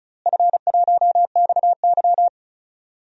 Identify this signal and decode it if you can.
Morse code, heard as a keyed tone, about 36 wpm, 700 Hz. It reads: F1XY
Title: DXCC-100 Call Signs - 4 Characters - Repeated 2x - 35wpm